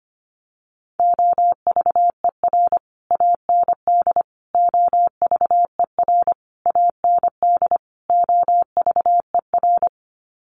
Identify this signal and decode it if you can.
Morse code, heard as a keyed tone, about 25 wpm, 700 Hz. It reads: OVER AND OVER AND OVER
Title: Sets of 5 Words Encore 25wpm